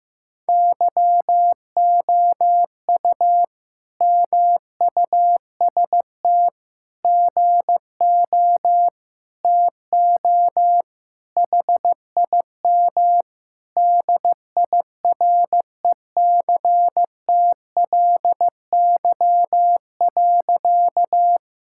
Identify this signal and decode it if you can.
Morse code, heard as a keyed tone, about 15 wpm, 700 Hz. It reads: YOU MUST GO TO HIM DIRECTLY.
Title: Sentences from Top 1000 Words - Review Part 5 - 15wpm